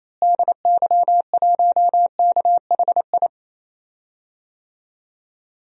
Morse code, heard as a keyed tone, about 28 wpm, 700 Hz. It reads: DY1KHI
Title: DXCC-100 Call Signs - 6 Characters - Repeated 2x - 28wpm